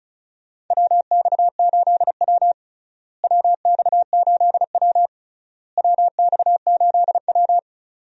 Morse code, heard as a keyed tone, about 36 wpm, 700 Hz. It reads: WX8W WX8W WX8W
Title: USA Call Signs - 4 Characters - Repeated 3x - 35wpm